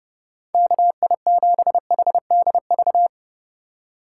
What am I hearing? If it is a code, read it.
Morse code, heard as a keyed tone, about 30 wpm, 700 Hz. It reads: KI7HDV
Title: USA Call Signs - 6 Characters - 3x Character Spacing - 30wpm